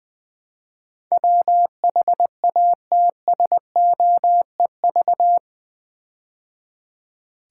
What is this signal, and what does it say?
Morse code, heard as a keyed tone, about 21 wpm, 700 Hz. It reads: WHATSOEV
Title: Build A Word - 4 Letters At A Time - 10-Letter Words - 20wpm